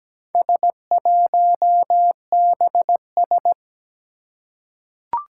Morse code, heard as a keyed tone, about 17 wpm, 700 Hz. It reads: S1BS
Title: Four Challenging Letter-Number Combinations 17wpm